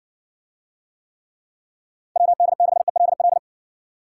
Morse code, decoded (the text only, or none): WD6ELL